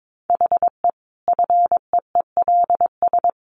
Morse code, heard as a keyed tone, about 22 wpm, 700 Hz. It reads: HE FEELS
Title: Easy Phrases 22wpm